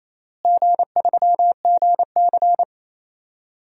Morse code, heard as a keyed tone, about 28 wpm, 700 Hz. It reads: G3GC
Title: DXCC-100 Call Signs - 4 Characters - Repeated 3x - 28wpm